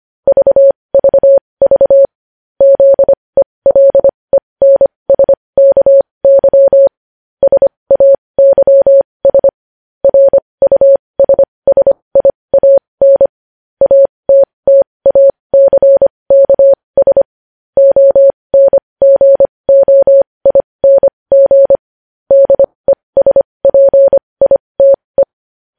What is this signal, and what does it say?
Morse code, heard as a keyed tone, about 25 wpm, spaced as for 20 wpm, 550 Hz. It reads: VVV ZELENSKY SAYS RUSSIAN ATTACKS ONGOING DESPITE
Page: news-headlines-in-morse-code-at-20-wpm